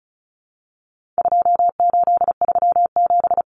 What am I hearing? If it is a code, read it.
Morse code, heard as a keyed tone, about 34 wpm, 700 Hz. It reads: 2837
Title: Common Cut Numbers - 4-Digit 35wpm